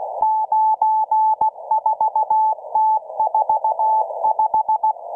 Morse code, heard as a keyed tone, about 16 wpm, 870 Hz, with radio noise. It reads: 94T45